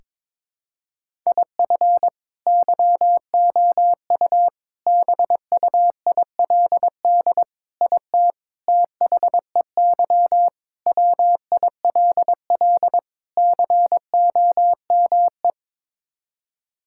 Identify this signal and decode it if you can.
Morse code, heard as a keyed tone, about 22 wpm, 700 Hz. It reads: IF YOU BUILD IT THEY WILL COME
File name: practice / 22wpm / Sets of 7 Words - 22wpm